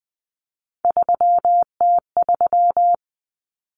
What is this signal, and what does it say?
Morse code, heard as a keyed tone, about 20 wpm, 700 Hz. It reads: 3T3
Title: Common Cut Numbers - 3-Digit 20wpm